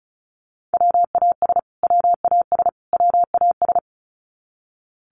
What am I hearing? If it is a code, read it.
Morse code, heard as a keyed tone, about 34 wpm, 700 Hz. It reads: WAS WAS WAS
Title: Top 500 Words Repeated 35wpm